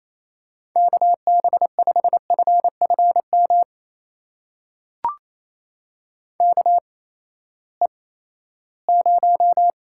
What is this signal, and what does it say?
Morse code, heard as a keyed tone, about 28 wpm, spaced as for 16 wpm, 700 Hz. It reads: KB5FFM KE0
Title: USA Call Signs - 6 Characters - 8x Character Spacing - 28wpm